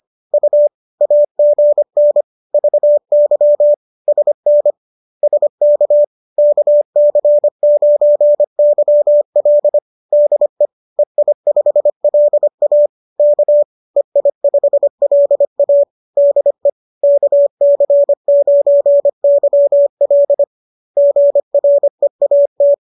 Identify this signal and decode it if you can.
Morse code, heard as a keyed tone, about 25 wpm, 580 Hz. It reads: U AGN VY SN SK KC9YL DE EI5LA K EI5LA DE KC9YL GREAT